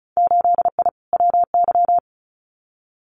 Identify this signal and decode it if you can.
Morse code, heard as a keyed tone, about 36 wpm, 700 Hz. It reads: 8I WY
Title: Winter Field Day Exchanges Encore 35wpm